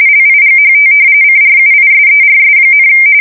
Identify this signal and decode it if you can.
RTTY, about 45.45 baud. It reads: LIGHT CRUISERS,
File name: fsk170 / battleshipnj